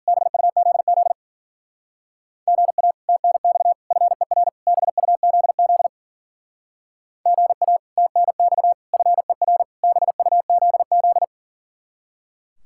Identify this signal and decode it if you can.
Morse code, heard as a keyed tone, about 50 wpm, 700 Hz. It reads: BUZZ GA TNX FER BUZZ GA TNX FER BUZZ
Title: Speed-Racing Common QSO Phrases - 40wpm